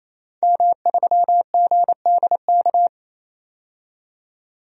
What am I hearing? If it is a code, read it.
Morse code, heard as a keyed tone, about 28 wpm, 700 Hz. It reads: M3GDK
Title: DXCC-100 Call Signs - 5 Characters - 28wpm